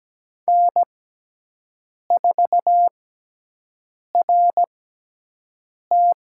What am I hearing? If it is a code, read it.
Morse code, heard as a keyed tone, about 17 wpm, spaced as for 6 wpm, 700 Hz. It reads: N4RT
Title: USA Call Signs - 6x Character Spacing - 17wpm